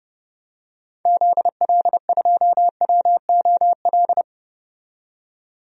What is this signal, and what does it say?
Morse code, heard as a keyed tone, about 32 wpm, 700 Hz. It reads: ZL2WOL
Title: DXCC-100 Call Signs - 6 Characters - 30wpm